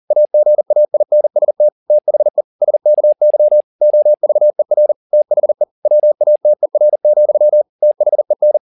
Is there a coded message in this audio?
Yes: Morse code from 0.1 to 8.6 s, heard as a keyed tone, about 40 wpm, 600 Hz. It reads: AGAINST THE SKY OVER THE WATER, THEN